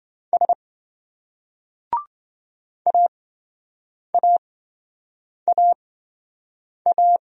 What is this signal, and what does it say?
Morse code, heard as a keyed tone, about 26 wpm, spaced as for 7 wpm, 700 Hz. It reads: S AAAA